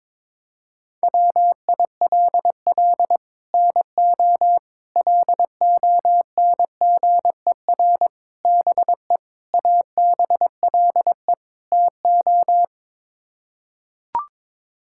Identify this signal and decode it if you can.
Morse code, heard as a keyed tone, about 23 wpm, 700 Hz. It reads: WILL NO LONGER BE ABLE TO
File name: practice / 22wpm / Sets of 6 Words - 22wpm